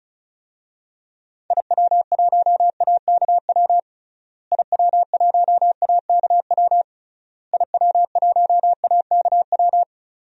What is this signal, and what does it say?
Morse code, heard as a keyed tone, about 36 wpm, 700 Hz. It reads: IW1AKW IW1AKW IW1AKW
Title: DXCC-100 Call Signs - 6 Characters - Repeated 3x - 35wpm